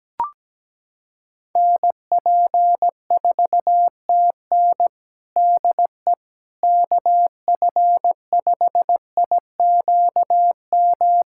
Morse code, heard as a keyed tone, about 17 wpm, 700 Hz. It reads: NP4TN DE KF5IQM
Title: USA Callsign DE Callsign 17wpm